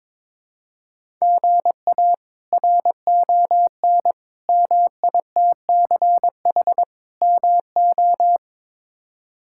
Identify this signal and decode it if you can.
Morse code, heard as a keyed tone, about 22 wpm, 700 Hz. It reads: GA RON MITCH MO
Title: K1USN SST QSO Practice - 22wpm